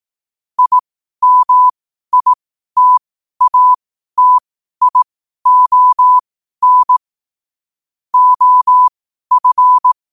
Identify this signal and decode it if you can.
Morse code, heard as a keyed tone, about 18 wpm, spaced as for 12 wpm, 1000 Hz. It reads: IMITATION OF
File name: qotdfiles / 12wpm